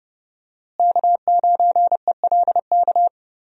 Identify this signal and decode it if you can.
Morse code, heard as a keyed tone, about 30 wpm, 700 Hz. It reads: K9ELK
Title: USA Call Signs - 5 Characters - Repeated 2x - 30wpm